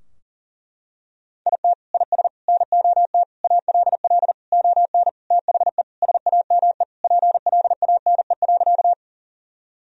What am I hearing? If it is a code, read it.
Morse code, heard as a keyed tone, about 40 wpm, 700 Hz. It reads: IT IS NOT ALL ON THE SAME PLANE.
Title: Sentences from Top 1000 Words - Review Part 3 - 40wpm